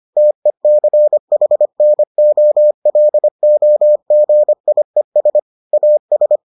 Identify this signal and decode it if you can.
Morse code, heard as a keyed tone, about 25 wpm, 600 Hz. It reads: TECHNOLOGIES AS